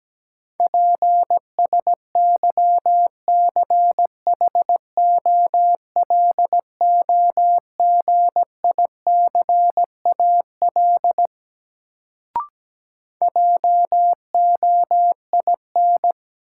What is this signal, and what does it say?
Morse code, heard as a keyed tone, about 17 wpm, 700 Hz. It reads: PSYCHOLOGICAL JOIN
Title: Sentences from Top 1400 Words - New Word Drill Alt - 17wpm